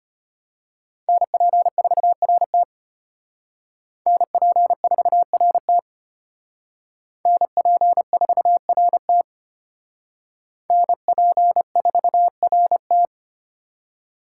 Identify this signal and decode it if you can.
Morse code, heard as a keyed tone, about 32 wpm, 700 Hz. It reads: NP4RT NP4RT NP4RT NP4RT